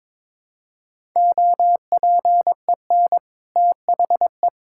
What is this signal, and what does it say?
Morse code, heard as a keyed tone, about 22 wpm, 690 Hz. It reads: OPEN THE
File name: practice / 22wpm / Easy Sentences with Hints - 22wpm